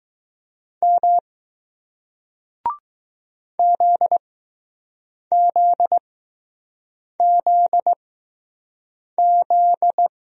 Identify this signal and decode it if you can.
Morse code, heard as a keyed tone, about 20 wpm, spaced as for 6 wpm, 700 Hz. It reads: M ZZZZ